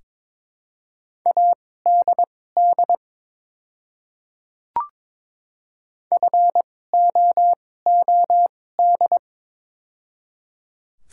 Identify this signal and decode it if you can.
Morse code, heard as a keyed tone, about 22 wpm, spaced as for 16 wpm, 700 Hz. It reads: ADD FOOD
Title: 2x Character Spacing - Top 300 Words - 22wpm